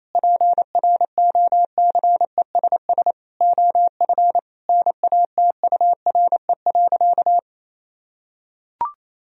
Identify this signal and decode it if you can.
Morse code, heard as a keyed tone, about 28 wpm, 700 Hz. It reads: PROCESS OF NATURE.
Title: Sentences from Top 1000 Words - Review Part 3 - 28wpm